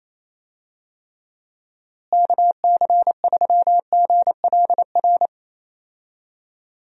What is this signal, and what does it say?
Morse code, heard as a keyed tone, about 28 wpm, 700 Hz. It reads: KC3GLR
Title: USA Call Signs - 6 Characters - Repeated 3x - 28wpm